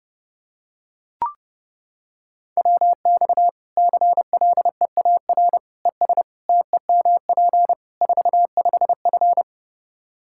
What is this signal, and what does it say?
Morse code, heard as a keyed tone, about 31 wpm, 700 Hz. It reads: WX CLEAR ES TEMP 45F
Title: Practice QSO - Exchange 2 - Info - 30wpm